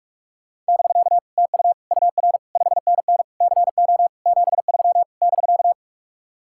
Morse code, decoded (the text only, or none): <BK> TU UR 5NN CO 73 <BK>